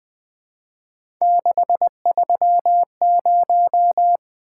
Morse code, heard as a keyed tone, about 20 wpm, 700 Hz. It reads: 630
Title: Three Letter-Number Combinations - 20wpm